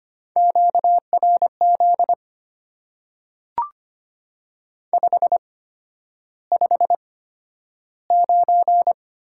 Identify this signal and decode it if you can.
Morse code, heard as a keyed tone, about 25 wpm, spaced as for 11 wpm, 690 Hz. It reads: QRZ 559
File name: Top 90 QSO Elements - 8x Character Spacing - 25wpm